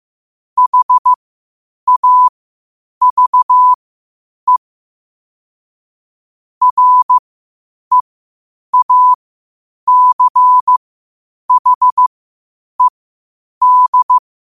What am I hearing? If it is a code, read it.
Morse code, heard as a keyed tone, about 15 wpm, spaced as for 8 wpm, 1000 Hz. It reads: HAVE REACHED
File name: qotdfiles / 8wpm